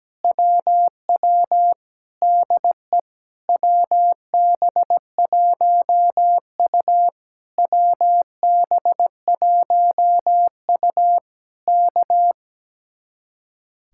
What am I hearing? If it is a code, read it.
Morse code, heard as a keyed tone, about 17 wpm, 700 Hz. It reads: WW DE WB1U WB1U K
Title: Speed-Racing Common QSO Phrases - 17wpm